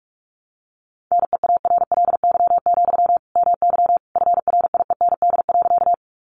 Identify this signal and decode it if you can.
Morse code, heard as a keyed tone, about 44 wpm, 700 Hz. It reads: NEARLY, MY FRIEND.
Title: Sentences from Top 1300 Words 45wpm